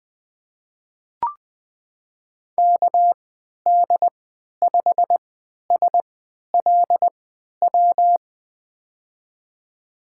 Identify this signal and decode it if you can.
Morse code, heard as a keyed tone, about 20 wpm, spaced as for 11 wpm, 700 Hz. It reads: KD5SLW
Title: USA Call Signs - 6 Characters - 3x Character Spacing - 20wpm